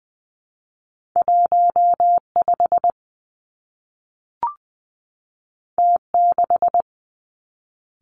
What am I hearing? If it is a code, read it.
Morse code, heard as a keyed tone, about 20 wpm, 700 Hz. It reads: 1 5 T 6
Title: Two Challenging Letter-Number Combinations 20wpm